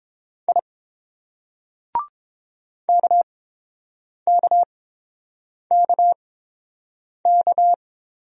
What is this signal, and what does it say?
Morse code, heard as a keyed tone, about 30 wpm, spaced as for 7 wpm, 700 Hz. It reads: I KKKK